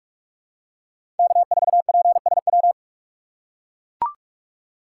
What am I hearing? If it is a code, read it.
Morse code, heard as a keyed tone, about 44 wpm, 700 Hz. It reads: K4PSW
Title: USA Call Signs - 5 Characters - 6x Character Spacing - 45wpm